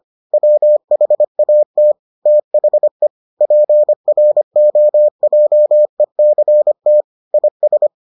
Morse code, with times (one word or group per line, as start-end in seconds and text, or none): 0.3-1.9 WHAT
2.3-3.1 THE
3.4-7.0 PROJECT
7.3-7.9 IS